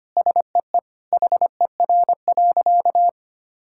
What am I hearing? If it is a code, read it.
Morse code, heard as a keyed tone, about 25 wpm, 700 Hz. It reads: SEE HER.
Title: Sentences from Top 1000 Words - Review Part 5 - 25wpm